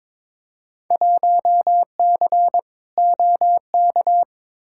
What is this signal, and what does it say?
Morse code, heard as a keyed tone, about 22 wpm, 700 Hz. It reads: 1C OK
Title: Field Day Exchanges Encore - 22wpm